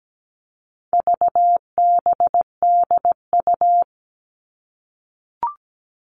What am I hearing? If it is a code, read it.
Morse code, heard as a keyed tone, about 17 wpm, 700 Hz. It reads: VBDU
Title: All Cut Numbers - 4-Digit 17wpm